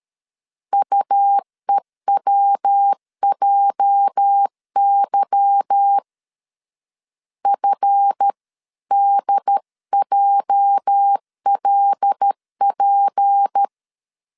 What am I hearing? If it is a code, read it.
Morse code, heard as a keyed tone, about 13 wpm, 800 Hz. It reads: UEWJY FDJLP
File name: LEZIONI / 8 YJ45 / 8 YJ 12wpm 2235